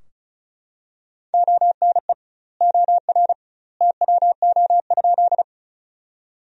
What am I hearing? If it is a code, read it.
Morse code, heard as a keyed tone, about 36 wpm, 700 Hz. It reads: ONE OR TWO?
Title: 2x Word-Spacing - Sentences from Top 100 Words - 35wpm